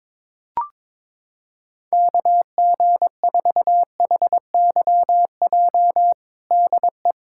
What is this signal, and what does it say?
Morse code, heard as a keyed tone, about 22 wpm, 700 Hz. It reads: KG4HYJ DE
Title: USA Callsign DE Callsign - 6 Characters - 22wpm